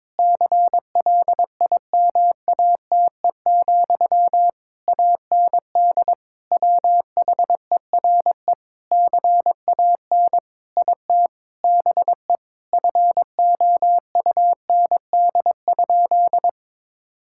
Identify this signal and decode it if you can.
Morse code, heard as a keyed tone, about 22 wpm, 700 Hz. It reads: CLIMATE, AND WHERE CAN IT BE FOUND?
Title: Sentences from Top 1600 Words 22wpm